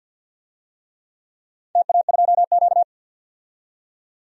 Morse code, heard as a keyed tone, about 52 wpm, 700 Hz. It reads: TA2Q
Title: DXCC-100 Call Signs - 4 Characters - 50wpm